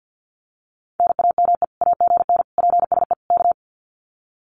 Morse code, heard as a keyed tone, about 50 wpm, 700 Hz. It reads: NAME AGN PSE K
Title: Common QSO Phrases Encore 50wpm